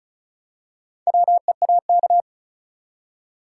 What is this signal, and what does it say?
Morse code, heard as a keyed tone, about 34 wpm, 700 Hz. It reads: WEAK
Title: Four Letter Words - 35wpm